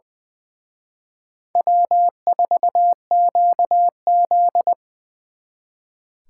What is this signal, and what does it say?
Morse code, heard as a keyed tone, about 24 wpm, 700 Hz. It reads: W4QZ
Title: Speed Racing - USA Call Signs 20wpm